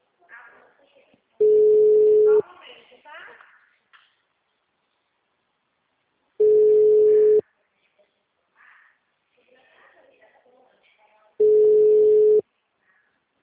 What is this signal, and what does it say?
Ringback tone (Europe, 425 Hz), with speech.
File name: out- -2050- - - .58296